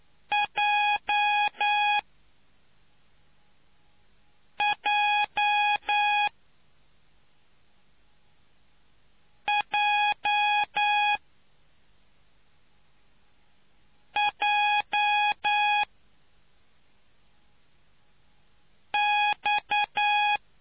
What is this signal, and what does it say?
Morse code, heard as a keyed tone, about 9 wpm, spaced as for 3 wpm, 830 Hz. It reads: JJJJX